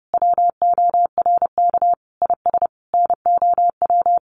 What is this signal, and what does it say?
Morse code, heard as a keyed tone, about 30 wpm, 700 Hz. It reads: WORK IS NOW
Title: Sentences from Top 400 Words 30wpm